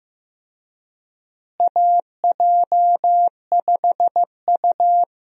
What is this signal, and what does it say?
Morse code, heard as a keyed tone, about 15 wpm, 700 Hz. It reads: AJ5U
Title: USA Call Signs - 4 Characters - Repeated 3x - 15wpm